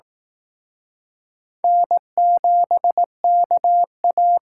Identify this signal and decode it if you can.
Morse code, heard as a keyed tone, about 18 wpm, 700 Hz. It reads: N7KA
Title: Speed Racing - USA 1x2 Call Signs 15wpm